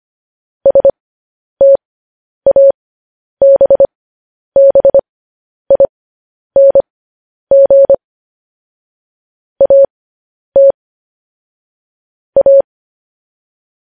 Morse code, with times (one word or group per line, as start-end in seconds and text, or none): 0.7-7.9 STABBING
9.6-10.7 AT
12.4-12.6 A